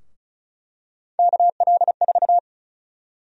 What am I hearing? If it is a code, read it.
Morse code, heard as a keyed tone, about 34 wpm, 700 Hz. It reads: KL4
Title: USA Call Sign Prefixes - 35wpm